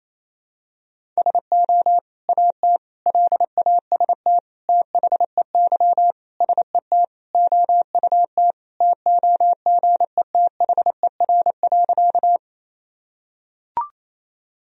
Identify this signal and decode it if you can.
Morse code, heard as a keyed tone, about 27 wpm, 700 Hz. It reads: SO AT LAST THEY SET OUT TOGETHER.
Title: Sentences from Top 1000 Words - Review Part 1 - 28wpm